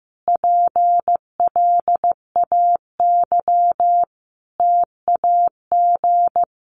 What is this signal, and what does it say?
Morse code, heard as a keyed tone, about 15 wpm, 700 Hz. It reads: PLAY TAG
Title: Additional Easy Phrases 15wpm